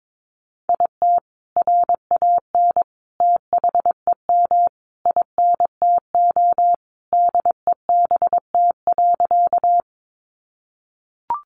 Morse code, heard as a keyed tone, about 22 wpm, 700 Hz. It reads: IT RAN THEM INTO DEBT.
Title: Sentences from Top 1300 Words 22wpm